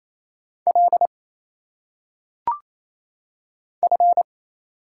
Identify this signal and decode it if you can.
Morse code, heard as a keyed tone, about 27 wpm, 700 Hz. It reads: L F
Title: Single Letters - Phonetic Pronunciation 28wpm